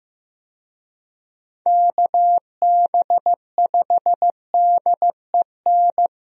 Morse code, heard as a keyed tone, about 15 wpm, 700 Hz. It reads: KB5DEN
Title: USA Call Signs - 6 Characters - Repeated 2x - 15wpm